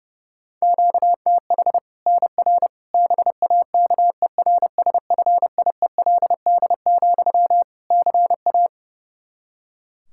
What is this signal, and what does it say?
Morse code, heard as a keyed tone, about 32 wpm, 700 Hz. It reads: QTH NR BAKERSFIELD, CA
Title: K1USN SST QSO Practice 30wpm